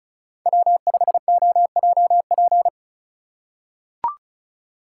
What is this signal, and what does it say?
Morse code, heard as a keyed tone, about 34 wpm, 700 Hz. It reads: W5OJP
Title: USA Call Signs - 5 Characters - 2x Character Spacing - 35wpm